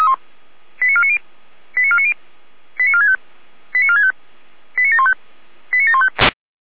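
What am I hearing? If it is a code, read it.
Five-tone selective call (ZVEI), heard as quick runs of five tones, tones of about 70 ms. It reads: 89490 89490 89465 89465 89715 89715
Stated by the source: ZVEI five-tone selective call 89533, SDR recording.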